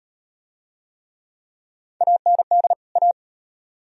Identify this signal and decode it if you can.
Morse code, heard as a keyed tone, about 38 wpm, 700 Hz. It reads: AND A